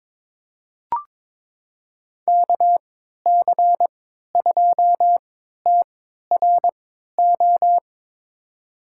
Morse code, heard as a keyed tone, about 22 wpm, spaced as for 13 wpm, 700 Hz. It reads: KC2TRO
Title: USA Call Signs - 6 Characters - 3x Character Spacing - 22wpm